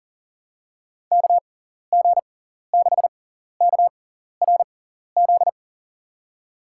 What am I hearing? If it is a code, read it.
Morse code, heard as a keyed tone, about 40 wpm, spaced as for 14 wpm, 700 Hz. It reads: KG6KRZ
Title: USA Call Signs - 6 Characters - 6x Character Spacing - 40wpm